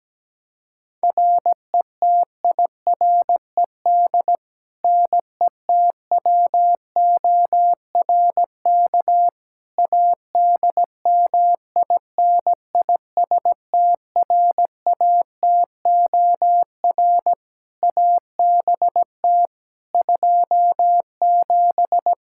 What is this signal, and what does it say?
Morse code, heard as a keyed tone, about 17 wpm, 700 Hz. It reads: RETIRED NETWORK ADMINISTRATOR ABT 27
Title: K1USN SST QSO Practice 17wpm